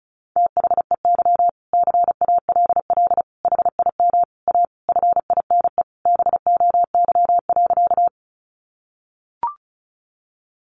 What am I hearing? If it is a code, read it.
Morse code, heard as a keyed tone, about 34 wpm, 700 Hz. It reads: THEY CALL HIM A FINE BOY.
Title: Sentences from Top 500 Words 35wpm